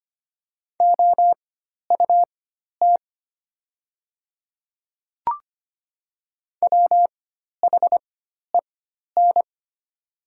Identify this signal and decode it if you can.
Morse code, heard as a keyed tone, about 25 wpm, spaced as for 12 wpm, 700 Hz. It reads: OUT WHEN
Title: 4x Character Spacing - Top 100 Words - 25wpm